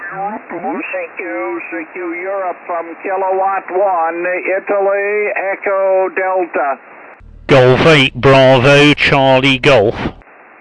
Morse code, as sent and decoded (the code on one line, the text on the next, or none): none